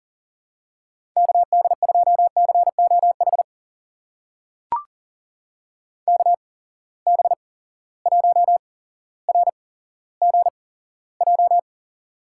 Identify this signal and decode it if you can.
Morse code, heard as a keyed tone, about 40 wpm, spaced as for 17 wpm, 700 Hz. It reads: KD2COH KB1RGJ
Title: USA Call Signs - 6 Characters - 8x Character Spacing - 40wpm